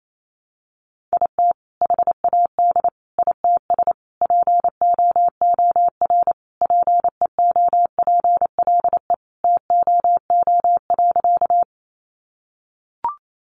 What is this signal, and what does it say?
Morse code, heard as a keyed tone, about 28 wpm, 700 Hz. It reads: IT HAD ITS POOR PEOPLE TOO.
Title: Sentences from Top 1000 Words 28wpm